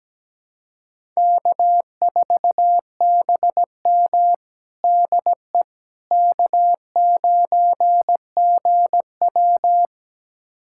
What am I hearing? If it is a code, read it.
Morse code, heard as a keyed tone, about 17 wpm, 700 Hz. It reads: K4BM DE K9GW
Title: USA Callsign DE Callsign - 4 Characters - 17wpm